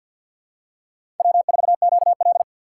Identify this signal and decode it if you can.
Morse code, heard as a keyed tone, about 52 wpm, 700 Hz. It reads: W4QL
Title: USA Call Signs - 6x Character Spacing - 50wpm